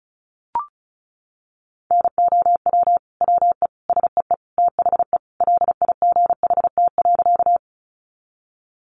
Morse code, heard as a keyed tone, about 34 wpm, 700 Hz. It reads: NOW WE SEE THE LIGHT.
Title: Sentences from Top 200 Words 35wpm